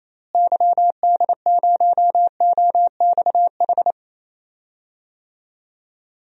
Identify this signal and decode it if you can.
Morse code, heard as a keyed tone, about 28 wpm, 700 Hz. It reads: YD0OXH
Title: DXCC-100 Call Signs - 6 Characters - Repeated 2x - 28wpm